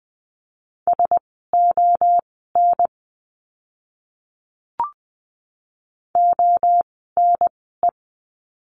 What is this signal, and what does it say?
Morse code, heard as a keyed tone, about 20 wpm, spaced as for 14 wpm, 700 Hz. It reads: SON ONE
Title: 068 Lesson 12f - Letter d - Words 20wpm